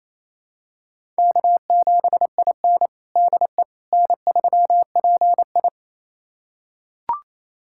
Morse code, heard as a keyed tone, about 28 wpm, 700 Hz. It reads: K7IN DE N3PI
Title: USA Callsign DE Callsign - 4 Characters - 28wpm